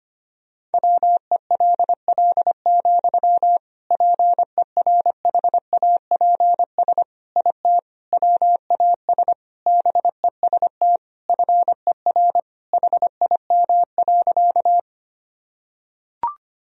Morse code, heard as a keyed tone, about 25 wpm, 700 Hz. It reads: WELL, PERHAPS IT WAS BEST FER HIM.
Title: Sentences from Top 500 Words and Abbreviations 25wpm